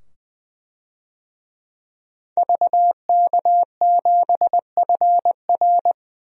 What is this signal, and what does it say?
Morse code, heard as a keyed tone, about 20 wpm, 700 Hz. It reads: VK7FR
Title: DXCC-100 Call Signs - 5 Characters - 20wpm